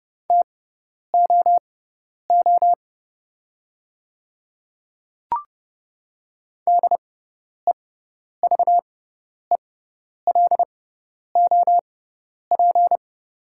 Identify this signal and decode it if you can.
Morse code, heard as a keyed tone, about 30 wpm, spaced as for 10 wpm, 700 Hz. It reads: TOO DEVELOP
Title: 6x Character Spacing - Top 500 Words - 30wpm